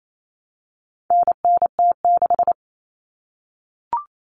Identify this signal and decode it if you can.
Morse code, heard as a keyed tone, about 28 wpm, 700 Hz. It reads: NNT6
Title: Common Cut Numbers - 4-Digit 28wpm